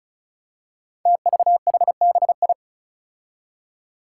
Morse code, heard as a keyed tone, about 36 wpm, 700 Hz. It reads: TVHBI
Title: Five Challenging Letter-Number Combinations 35wpm